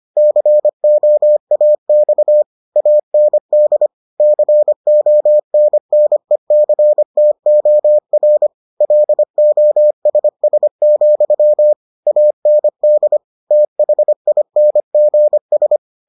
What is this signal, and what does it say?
Morse code, heard as a keyed tone, about 25 wpm, 600 Hz. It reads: COAX AND CONNECTOR LOSS, AND THINGS